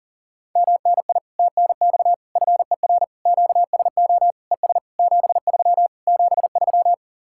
Morse code, heard as a keyed tone, about 54 wpm, spaced as for 47 wpm, 700 Hz. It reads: MNI TNX FER QSO ES 73 73
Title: Speed-Racing Common QSO Phrases - 40wpm